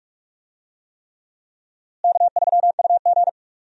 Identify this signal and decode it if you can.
Morse code, heard as a keyed tone, about 41 wpm, 700 Hz. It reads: K3UG
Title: USA Call Signs - 4 Characters - Repeated 3x - 45wpm